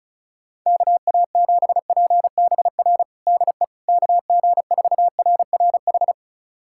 Morse code, heard as a keyed tone, about 34 wpm, 700 Hz. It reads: KA7PBR DE KG4RRH
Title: USA Callsign DE Callsign - 6 Characters - 35wpm